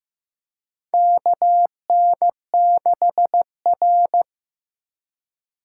Morse code, heard as a keyed tone, about 15 wpm, 700 Hz. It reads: KN6R
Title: USA Call Signs - 4 Characters - 2x Character Spacing - 15wpm